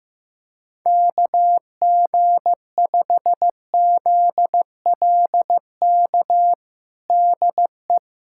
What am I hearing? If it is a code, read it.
Morse code, heard as a keyed tone, about 16 wpm, 700 Hz. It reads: KG5ZLK DE
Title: USA Callsign DE Callsign - 6 Characters - 15wpm